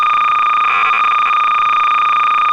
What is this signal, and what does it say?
Cassette data (Kansas City type).
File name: 300baud